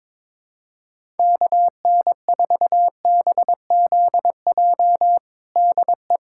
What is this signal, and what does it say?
Morse code, heard as a keyed tone, about 22 wpm, 700 Hz. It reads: KN4BZJ DE
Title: USA Callsign DE Callsign - 6 Characters - 22wpm